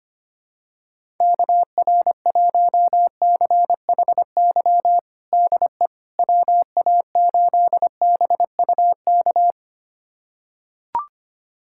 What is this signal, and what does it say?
Morse code, heard as a keyed tone, about 25 wpm, 700 Hz. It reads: KR1CHY DE WA8BUK
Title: USA Callsign DE Callsign - 6 Characters - 25wpm